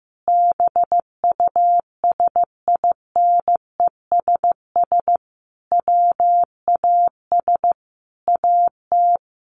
Morse code, heard as a keyed tone, about 14 wpm, 700 Hz. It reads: BUSINESS WAS AT
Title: Sentences from Top 1300 Words 15wpm